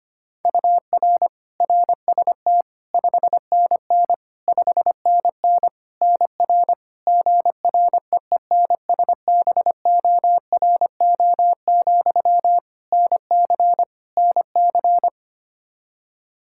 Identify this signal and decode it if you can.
Morse code, heard as a keyed tone, about 25 wpm, 700 Hz. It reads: UR RST 5NN 5NN NR GREENSBORO, NC NC
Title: Practice QSO - Exchange 1 - Beginning - 25wpm